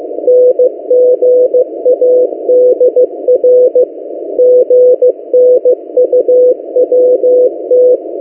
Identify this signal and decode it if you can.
Morse code, heard as a keyed tone, about 16 wpm, 500 Hz. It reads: NGADR GNUWT
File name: M08a- -1355utc-8096khz